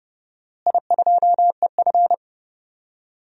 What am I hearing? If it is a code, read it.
Morse code, heard as a keyed tone, about 28 wpm, 700 Hz. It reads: I2EF
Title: DXCC-100 Call Signs - 4 Characters - Repeated 3x - 30wpm